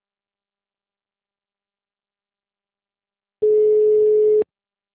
Ringback tone (Europe, 425 Hz).